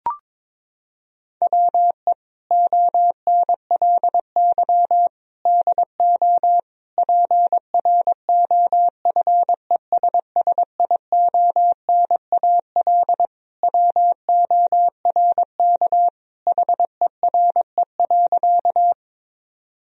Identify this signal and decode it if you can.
Morse code, heard as a keyed tone, about 22 wpm, 700 Hz. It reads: WE ONLY DO PROFESSIONAL WORK HERE.
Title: Sentences from Top 1000 Words - Review Part 4 - 22wpm